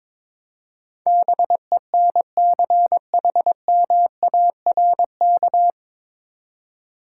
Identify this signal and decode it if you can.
Morse code, heard as a keyed tone, about 22 wpm, 700 Hz. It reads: BENCHMARK
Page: Morse Code Ninja practice 22wpm